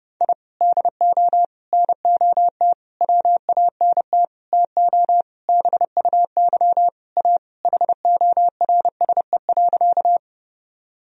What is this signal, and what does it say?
Morse code, heard as a keyed tone, about 30 wpm, 700 Hz. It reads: I DO NOT WANT TO BUY A HORSE.
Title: Sentences from Top 1000 Words - Review Part 4 - 30wpm